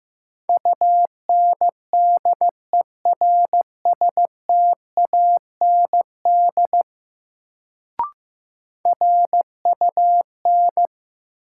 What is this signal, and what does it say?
Morse code, heard as a keyed tone, about 15 wpm, 700 Hz. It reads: UNDERSTAND RUN
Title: Sets of 4 Words - New Word Drill - 15wpm